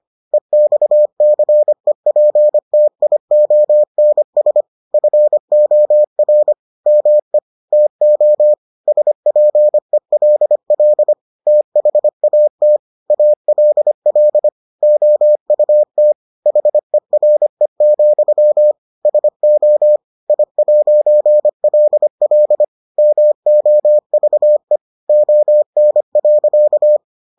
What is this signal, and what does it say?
Morse code, heard as a keyed tone, about 25 wpm, 600 Hz. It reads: EXCEPTIONS FOR ME TO SPELL THAT ALL OUT HERE, SO I'LL MOVE ON.